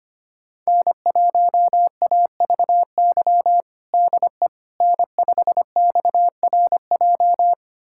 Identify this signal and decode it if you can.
Morse code, heard as a keyed tone, about 25 wpm, 700 Hz. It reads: N1AVY DE N5XRJ
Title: USA Callsign DE Callsign - 5 Characters - 25wpm